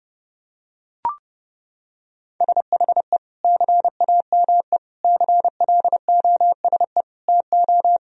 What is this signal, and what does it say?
Morse code, heard as a keyed tone, about 32 wpm, 700 Hz. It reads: SHE CAME CLOSE TO
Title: Sentences from Top 1000 Words - Review Part 1 - 30wpm